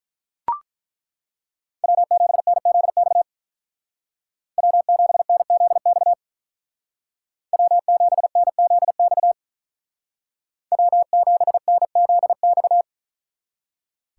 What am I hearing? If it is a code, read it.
Morse code, heard as a keyed tone, about 43 wpm, 700 Hz. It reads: W7NZX W7NZX W7NZX W7NZX
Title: Speed Racing - USA Call Signs 35wpm